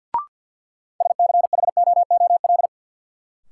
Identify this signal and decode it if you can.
Morse code, heard as a keyed tone, about 48 wpm, 700 Hz. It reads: SCHOOL
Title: Sets of 2 Words - New Word Drill - 50wpm